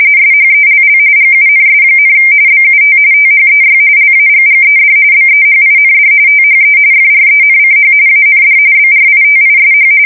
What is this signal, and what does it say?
RTTY, about 45.45 baud. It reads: SHELL GUNS, BUT THEY LAGGED BEHIND THE FRENCH AND BRITISH
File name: fsk170 / usnavy2